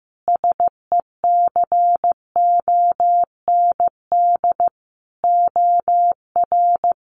Morse code, heard as a keyed tone, about 15 wpm, 700 Hz. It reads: SECOND OR
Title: Sentences from Top 1500 Words 15wpm